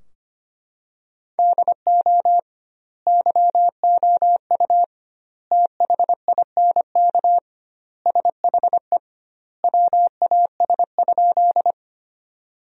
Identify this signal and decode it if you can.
Morse code, heard as a keyed tone, about 25 wpm, 700 Hz. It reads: DO YOU THINK SHE WAS?
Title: 2x Word-Spacing - Sentences from Top 200 Words - 25wpm